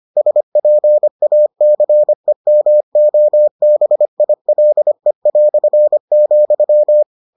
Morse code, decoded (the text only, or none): SPACEMOBILE",